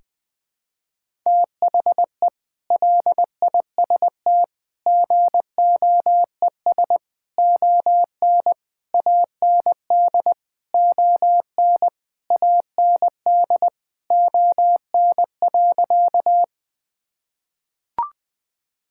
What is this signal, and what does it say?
Morse code, heard as a keyed tone, about 20 wpm, 700 Hz. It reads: THE LIST GOES ON AND ON AND ON.
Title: Sentences with 8-Word Phrases - Spoken Beginning - 20wpm